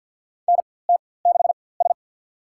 Morse code, heard as a keyed tone, about 50 wpm, spaced as for 23 wpm, 700 Hz. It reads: NT6S
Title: USA Call Signs - 4x Character Spacing - 50wpm